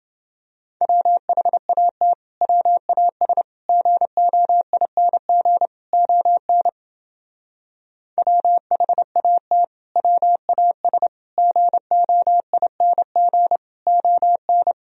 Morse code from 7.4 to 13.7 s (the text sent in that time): WHAT WAS GOING